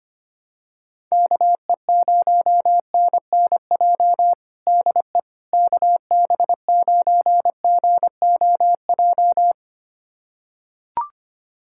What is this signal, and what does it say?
Morse code, heard as a keyed tone, about 25 wpm, 700 Hz. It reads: KE0NNJ DE KB9GOJ
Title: USA Callsign DE Callsign - 6 Characters - 25wpm